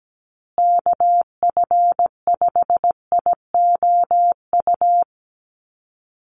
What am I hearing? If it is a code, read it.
Morse code, heard as a keyed tone, about 17 wpm, 700 Hz. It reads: KF5IOU
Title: USA Call Signs Repeated 2x 17wpm